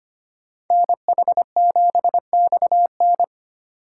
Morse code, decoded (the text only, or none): NH7XN